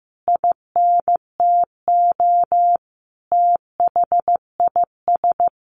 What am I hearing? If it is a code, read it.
Morse code, heard as a keyed tone, about 15 wpm, 700 Hz. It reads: INTO THIS
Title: Sentences from Top 600 Words 15wpm